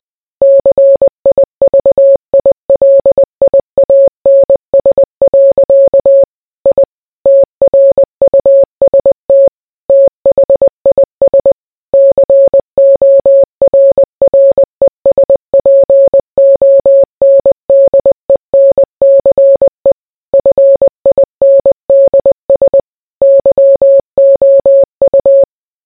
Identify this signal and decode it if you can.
Morse code, heard as a keyed tone, about 20 wpm, 550 Hz. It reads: CIVILIANS. I TRUST THIS CORRESPONDENCE FINDS YOU